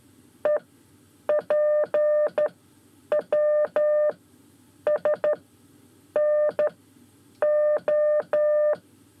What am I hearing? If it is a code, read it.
Morse code, heard as a keyed tone, about 13 wpm, spaced as for 8 wpm, 600 Hz. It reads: EPWSNO
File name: 5 Lezione CW 1